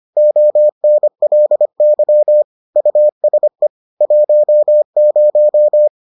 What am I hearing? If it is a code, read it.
Morse code, heard as a keyed tone, about 25 wpm, 600 Hz. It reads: ONLY USE 10